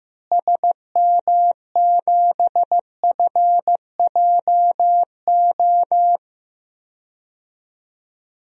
Morse code, heard as a keyed tone, about 16 wpm, 700 Hz. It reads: SM7FJO
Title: DXCC-100 Call Signs - 6 Characters - Repeated 2x - 15wpm